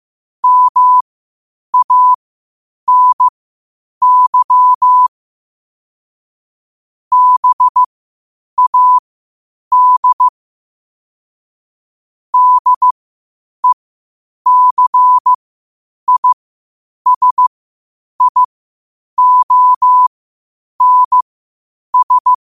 Morse code, heard as a keyed tone, about 15 wpm, spaced as for 8 wpm, 1000 Hz. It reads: MANY BAD DECISIONS
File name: qotdfiles / 8wpm